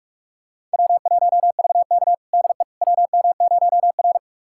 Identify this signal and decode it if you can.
Morse code, heard as a keyed tone, about 46 wpm, 700 Hz. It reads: W1VK DE WM0R
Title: USA Callsign DE Callsign - 4 Characters - 45wpm